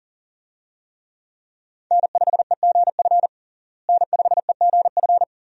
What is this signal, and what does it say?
Morse code, heard as a keyed tone, about 40 wpm, 700 Hz. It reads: N5EGF N5EGF
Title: USA Call Signs - 5 Characters - Repeated 2x - 40wpm